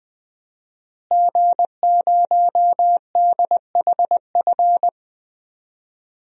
Morse code, heard as a keyed tone, about 20 wpm, 700 Hz. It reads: G0DHF
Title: DXCC-100 Call Signs - 5 Characters - Repeated 2x - 20wpm